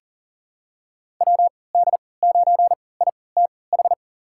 Morse code, heard as a keyed tone, about 40 wpm, spaced as for 23 wpm, 700 Hz. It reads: WD9ITH
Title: USA Call Signs - 6 Characters - 3x Character Spacing - 40wpm